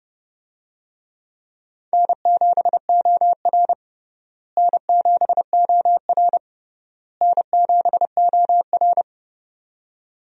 Morse code, heard as a keyed tone, about 32 wpm, 700 Hz. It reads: N7OR N7OR N7OR
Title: USA Call Signs - 4 Characters - Repeated 3x - 30wpm